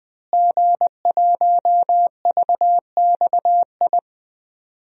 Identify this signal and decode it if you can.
Morse code, heard as a keyed tone, about 20 wpm, 700 Hz. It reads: G1VXI
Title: DXCC-100 Call Signs - 5 Characters - Repeated 3x - 20wpm